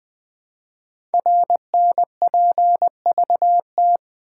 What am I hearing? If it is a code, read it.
Morse code, heard as a keyed tone, about 20 wpm, 700 Hz. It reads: RNPVT
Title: Code Groups - 5 Letters - 20wpm